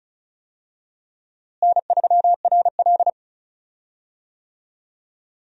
Morse code, heard as a keyed tone, about 36 wpm, 700 Hz. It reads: N3RL
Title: USA Call Signs - 4 Characters - Repeated 3x - 35wpm